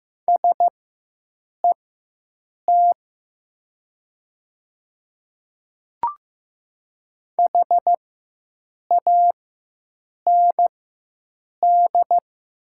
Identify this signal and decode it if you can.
Morse code, heard as a keyed tone, about 15 wpm, spaced as for 7 wpm, 700 Hz. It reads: SET HAND
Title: 4x Character Spacing - Top 300 Words - 15wpm